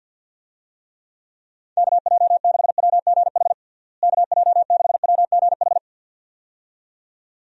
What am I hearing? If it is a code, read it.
Morse code, heard as a keyed tone, about 50 wpm, 700 Hz. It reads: KJ6WGH KJ6WGH
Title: USA Call Signs - 6 Characters - Repeated 2x - 50wpm